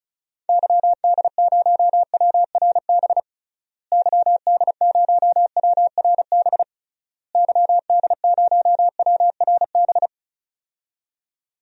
Morse code, heard as a keyed tone, about 36 wpm, 700 Hz. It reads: YD0WRB YD0WRB YD0WRB
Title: DXCC-100 Call Signs - 6 Characters - Repeated 3x - 35wpm